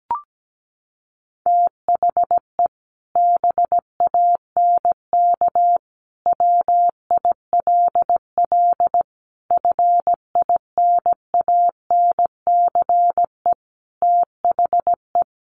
Morse code, heard as a keyed tone, about 17 wpm, 700 Hz. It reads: THE BANK WILL FINANCE THE
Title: Sentences from Top 1500 Words 17wpm